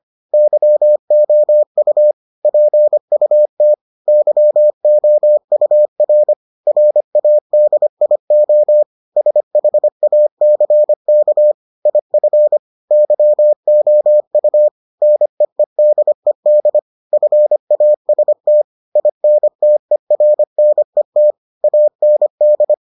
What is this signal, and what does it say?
Morse code, heard as a keyed tone, about 25 wpm, 600 Hz. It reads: YOU PUT YOUR RADIO SHACK IF YOU NEEDED FAST INTERNET AND